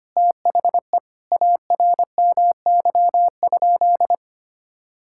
Morse code, heard as a keyed tone, about 25 wpm, 700 Hz. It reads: THE ARMY?
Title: Sentences from Top 1000 Words - Review Part 5 - 25wpm